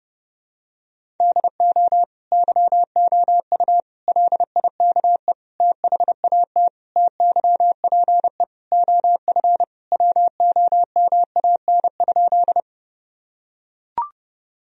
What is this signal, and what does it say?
Morse code, heard as a keyed tone, about 31 wpm, 700 Hz. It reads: DO YOU LIKE THAT TYPE OF WOMAN?
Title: Sentences from Top 1000 Words - Review Part 3 - 30wpm